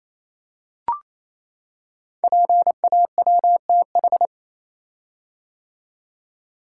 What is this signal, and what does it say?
Morse code, heard as a keyed tone, about 27 wpm, 700 Hz. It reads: PAWTH
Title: Code Groups - 5 Letters - 28wpm